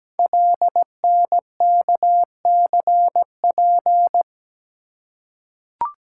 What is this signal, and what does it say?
Morse code, heard as a keyed tone, about 17 wpm, 700 Hz. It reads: LNKCP
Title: Code Groups - 5 Letter:Numbers - 17wpm